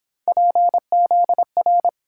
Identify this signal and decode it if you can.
Morse code, heard as a keyed tone, about 29 wpm, 700 Hz. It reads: PZR